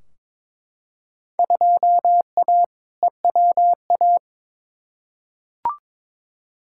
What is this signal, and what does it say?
Morse code, heard as a keyed tone, about 22 wpm, 700 Hz. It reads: 2A EWA
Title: Field Day Exchanges - 22wpm